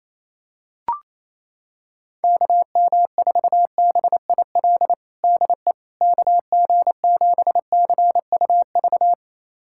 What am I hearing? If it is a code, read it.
Morse code, heard as a keyed tone, about 28 wpm, 700 Hz. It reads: KM4BIL DE KG7CUV
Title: USA Callsign DE Callsign - 6 Characters - 28wpm